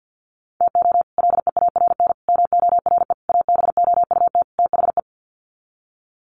Morse code, heard as a keyed tone, about 48 wpm, 700 Hz. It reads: TO LEARN MORE ABOUT THE
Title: Sets of 5 Words Encore2 50wpm